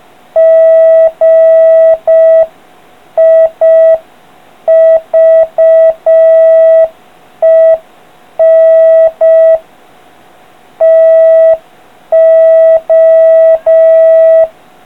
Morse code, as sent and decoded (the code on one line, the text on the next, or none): --. .. ...- . -. / - ---
GIVEN TO